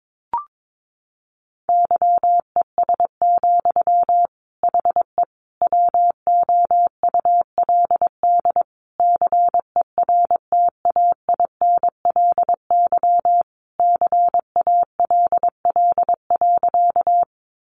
Morse code, heard as a keyed tone, about 22 wpm, 700 Hz. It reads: YES, HE WOULD CERTAINLY CALL.
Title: Sentences from Top 900 Words 22wpm